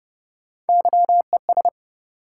Morse code, heard as a keyed tone, about 30 wpm, 700 Hz. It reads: YES
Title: Sentences from Top 500 Words - New Word Drill - 30wpm